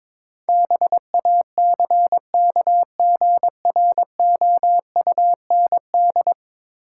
Morse code, heard as a keyed tone, about 22 wpm, 700 Hz. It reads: BACKGROUND
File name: practice / 22wpm / 10 Letter Words - First 3 Letters Given - 22wpm